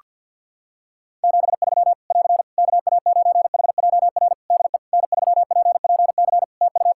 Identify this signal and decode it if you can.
Morse code, heard as a keyed tone, about 50 wpm, 700 Hz. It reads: 73 <AR> KA9HJR DE N3PPC TU
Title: Practice QSO - Exchange 4 - Ending - 50wpm